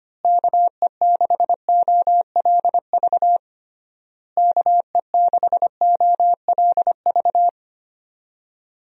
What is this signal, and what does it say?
Morse code, heard as a keyed tone, about 25 wpm, 700 Hz. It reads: KE6OLV KE6OLV
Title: USA Call Signs - 6 Characters - Repeated 3x - 25wpm